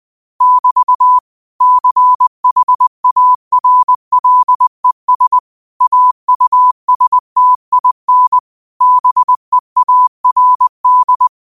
Morse code, heard as a keyed tone, about 20 wpm, 1000 Hz. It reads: <BT> CHARLES AUSTIN BEARD
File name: qotdfiles / 20wpm